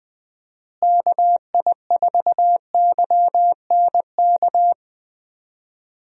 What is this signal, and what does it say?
Morse code, heard as a keyed tone, about 20 wpm, 700 Hz. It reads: KI4YNK
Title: USA Call Signs - 6 Characters - Repeated 3x - 20wpm